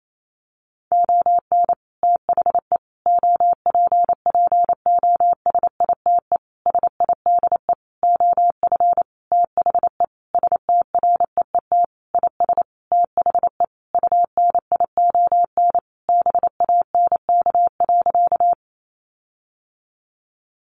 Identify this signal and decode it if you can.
Morse code, heard as a keyed tone, about 28 wpm, 700 Hz. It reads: ON THE OPPOSITE SIDE OF THE STREET IS THE UNION BANK.
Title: Sentences from Top 1400 Words 28wpm